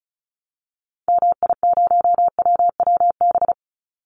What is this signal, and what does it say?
Morse code, heard as a keyed tone, about 34 wpm, 700 Hz. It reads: MI0WWB
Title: Call Signs Repeated 2x - 35wpm